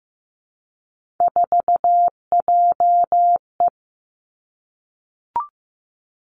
Morse code, heard as a keyed tone, about 15 wpm, 700 Hz. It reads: 4JE
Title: Three Challenging Letter-Number Combinations 15wpm